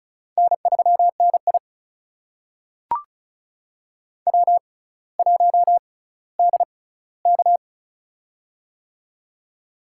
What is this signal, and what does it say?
Morse code, heard as a keyed tone, about 36 wpm, spaced as for 18 wpm, 700 Hz. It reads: N3NI W1DK
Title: USA Call Signs - 6x Character Spacing - 35wpm